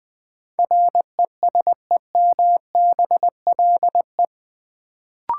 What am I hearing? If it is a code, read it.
Morse code, heard as a keyed tone, about 20 wpm, 700 Hz. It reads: RESEMBLE
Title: Build A Word - 2 Letters At A Time - 8-Letter Words - 20wpm